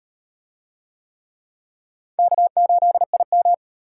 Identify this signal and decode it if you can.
Morse code, heard as a keyed tone, about 37 wpm, 700 Hz. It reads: K8IM